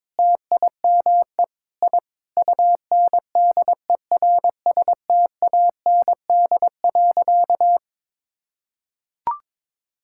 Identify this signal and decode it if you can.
Morse code, heard as a keyed tone, about 22 wpm, 700 Hz. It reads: TIME I UNDERSTAND.
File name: practice / 22wpm / Sentences with 4-Word Phrases - 22wpm